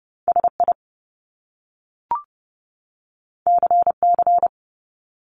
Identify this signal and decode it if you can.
Morse code, heard as a keyed tone, about 30 wpm, 700 Hz. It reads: SI CC
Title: Top Two Letter Combinations in Words 30wpm